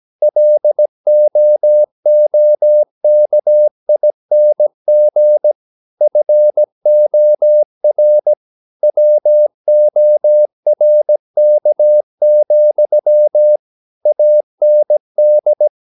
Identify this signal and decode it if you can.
Morse code, heard as a keyed tone, about 17 wpm, 600 Hz. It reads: LOOKING FOR WORK, AND
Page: Romanen Never af Ken Follet på engelsk.